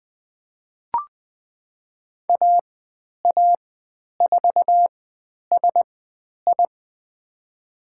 Morse code, heard as a keyed tone, about 20 wpm, spaced as for 10 wpm, 700 Hz. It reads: AA4SI
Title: 109 Lesson 18g - Letter m - Call Signs 20wpm